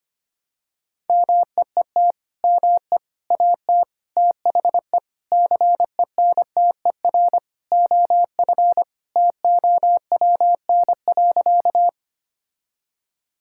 Morse code, heard as a keyed tone, about 25 wpm, 700 Hz. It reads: MEET ME AT THE CENTER OF TOWN.
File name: practice / 25wpm / Sentences with 3-Word Phrases - 25wpm